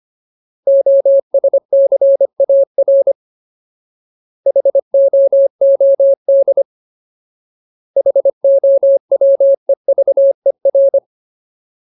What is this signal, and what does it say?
Morse code, heard as a keyed tone, about 25 wpm, 550 Hz. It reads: OSCAR HOOD HOWEVER
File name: Ultimate Word List / 25WPM Ultimate